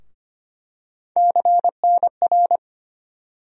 Morse code, heard as a keyed tone, about 25 wpm, 700 Hz. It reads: CNR
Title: USA Call Sign Postfixes - 3 Letters 25wpm